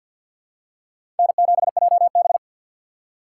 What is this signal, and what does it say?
Morse code, heard as a keyed tone, about 52 wpm, 700 Hz. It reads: N7JB
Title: USA Call Signs - 4 Characters - 3x Character Spacing - 50wpm